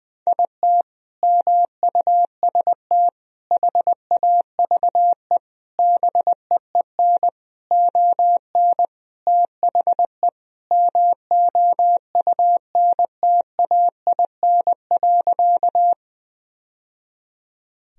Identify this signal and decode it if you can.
Morse code, heard as a keyed tone, about 20 wpm, 700 Hz. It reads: IT MUST HAVE BEEN ON THE MOUNTAIN.
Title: Sentences from Top 1000 Words - Review Part 1 - 20wpm